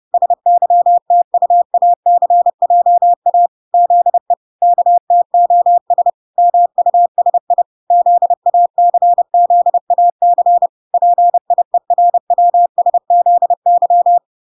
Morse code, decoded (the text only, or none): SYTUACJA ZE KTOS MUSI ZACZAC PIERWSZY